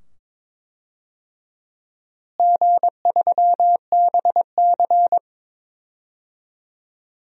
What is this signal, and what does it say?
Morse code, heard as a keyed tone, about 22 wpm, 700 Hz. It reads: G3BC
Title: DXCC-100 Call Signs - 4 Characters - Repeated 2x - 22wpm